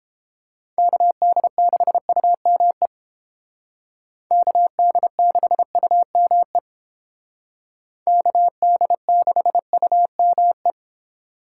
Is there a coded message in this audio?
Yes: Morse code from 0.8 to 10.7 s, heard as a keyed tone, about 30 wpm, 700 Hz. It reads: KD6UME KD6UME KD6UME